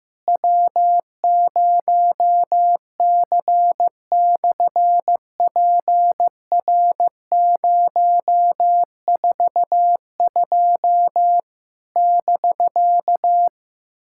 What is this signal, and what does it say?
Morse code, heard as a keyed tone, about 15 wpm, 700 Hz. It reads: W0C/PR042 <BK>
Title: SOTA QSO Practice 15wpm